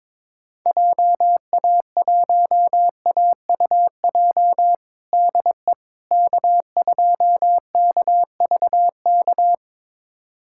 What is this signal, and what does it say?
Morse code, heard as a keyed tone, about 22 wpm, 700 Hz. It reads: JA1AUJ DE K2KVK
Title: USA Callsign DE Callsign 22wpm